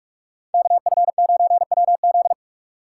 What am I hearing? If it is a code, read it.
Morse code, heard as a keyed tone, about 44 wpm, 690 Hz. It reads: KF9WZ
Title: USA Call Signs - 4x Character Spacing - 45wpm